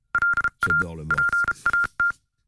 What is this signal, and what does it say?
Morse code, heard as a keyed tone, about 33 wpm, 1400 Hz, with speech. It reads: LA PAT